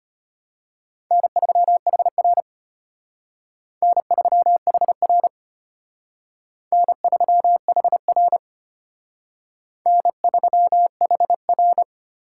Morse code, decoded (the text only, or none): N3HR N3HR N3HR N3HR